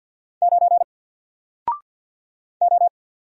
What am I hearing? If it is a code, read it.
Morse code, heard as a keyed tone, about 50 wpm, 700 Hz. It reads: 9 O
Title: Single Letter-Number 50wpm